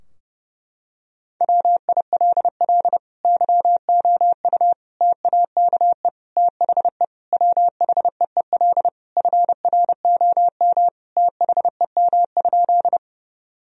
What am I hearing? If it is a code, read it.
Morse code, heard as a keyed tone, about 32 wpm, 700 Hz. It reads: WILL YOU TAKE THE WHEEL FROM THEM?
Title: Sentences from Top 1000 Words - Review Part 3 - 30wpm